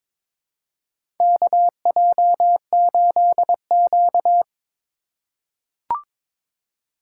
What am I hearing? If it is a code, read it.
Morse code, heard as a keyed tone, about 22 wpm, 700 Hz. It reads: KJ8Q
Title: USA Call Signs - 4 Characters - 2x Character Spacing - 22wpm